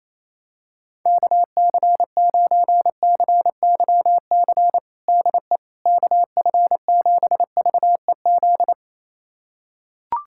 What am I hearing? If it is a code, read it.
Morse code, heard as a keyed tone, about 28 wpm, 700 Hz. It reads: KC9CYC DE KF7VEZ
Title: USA Callsign DE Callsign 28wpm